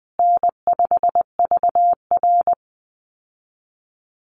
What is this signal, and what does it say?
Morse code, heard as a keyed tone, about 20 wpm, 700 Hz. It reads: N5VR
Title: Easy Call Signs Repeated 2x - 20wpm